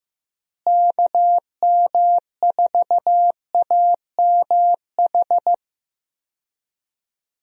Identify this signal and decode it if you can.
Morse code, heard as a keyed tone, about 15 wpm, 700 Hz. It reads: KM4AMH
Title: USA Call Signs - 6 Characters - Repeated 3x - 15wpm